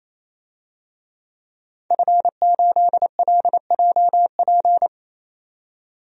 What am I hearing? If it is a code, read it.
Morse code, heard as a keyed tone, about 27 wpm, 700 Hz. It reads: F8LJP
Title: DXCC-100 Call Signs - 5 Characters - 28wpm